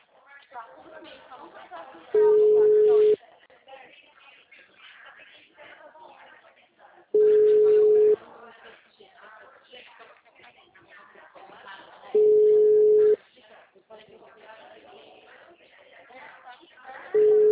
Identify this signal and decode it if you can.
Ringback tone (Europe, 425 Hz), with speech.
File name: out- -2003- - - .18255